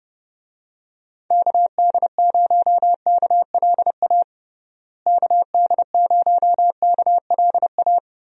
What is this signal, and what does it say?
Morse code, heard as a keyed tone, about 30 wpm, 700 Hz. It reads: KD0KLA KD0KLA
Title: USA Call Signs - 6 Characters - Repeated 3x - 30wpm